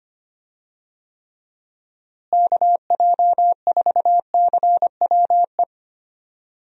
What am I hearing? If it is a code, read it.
Morse code, heard as a keyed tone, about 25 wpm, 700 Hz. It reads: KJ4CWE
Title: USA Call Signs - 6 Characters - Repeated 2x - 25wpm